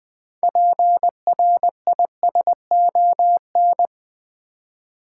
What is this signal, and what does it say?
Morse code, heard as a keyed tone, about 20 wpm, 700 Hz. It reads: PRISON
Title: 6 Letter Words - First 3 Letters Given - 20wpm